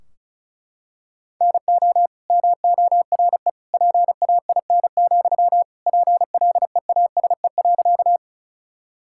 Morse code, heard as a keyed tone, about 36 wpm, 700 Hz. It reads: NO MORE PAIN, PLEASE.
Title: Sentences from Top 1600 Words 35wpm